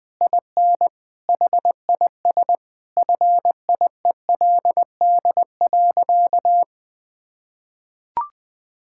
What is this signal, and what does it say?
Morse code, heard as a keyed tone, about 21 wpm, 700 Hz. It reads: IN HIS FIELD.
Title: Sentences with 7-Word Phrases - 20wpm